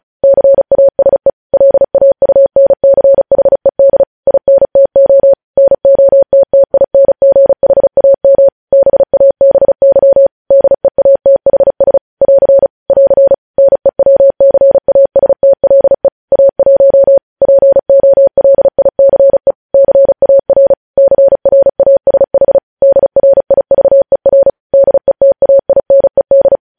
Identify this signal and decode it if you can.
Morse code, heard as a keyed tone, about 36 wpm, 550 Hz. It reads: CASE LAUNCHED INTO NOTTINGHAM BABY DEATHS <AR> <AR> NEWCASTLE A1 POLICE CAR CRASH DRIVER DETAINED
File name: podcast / News headlines-1x-35x35